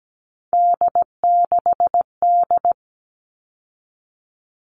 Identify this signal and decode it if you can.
Morse code, heard as a keyed tone, about 17 wpm, 700 Hz. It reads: D6D
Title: All Cut Numbers - 3-Digit 17wpm